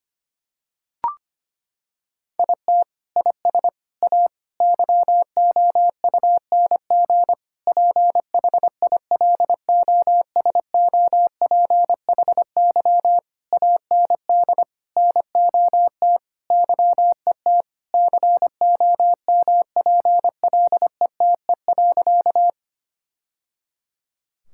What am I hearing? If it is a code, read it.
Morse code, heard as a keyed tone, about 25 wpm, 700 Hz. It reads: IT IS A YOUNG PHILOSOPHY AND NOT YET COMPLETE.
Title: Sentences from Top 1000 Words - Review Part 6 - 25wpm